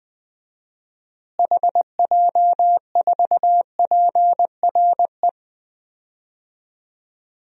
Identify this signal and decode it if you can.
Morse code, heard as a keyed tone, about 20 wpm, 700 Hz. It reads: HJ4PRE
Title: DXCC-100 Call Signs - 6 Characters - Repeated 2x - 20wpm